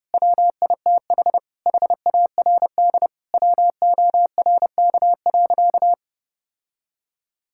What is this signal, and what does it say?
Morse code, heard as a keyed tone, about 30 wpm, 700 Hz. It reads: WITH HARD WORK.
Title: Sentences from Top 1000 Words - Review Part 5 - 30wpm